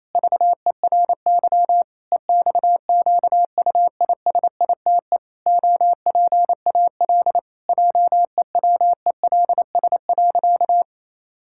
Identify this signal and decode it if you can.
Morse code, heard as a keyed tone, about 28 wpm, 700 Hz. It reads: VERY EXQUISITE OPAL JEWELS.
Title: Pangram Challenge v2 - 28wpm